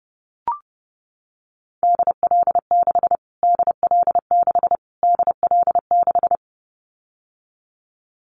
Morse code, heard as a keyed tone, about 30 wpm, 700 Hz. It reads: DL6 DL6 DL6
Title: Callsign Prefixes for Top 25 DXCC - Repeated 3x - 30wpm